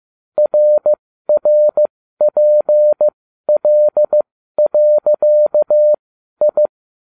Morse code, heard as a keyed tone, about 15 wpm, spaced as for 12 wpm, 600 Hz. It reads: RRPL.I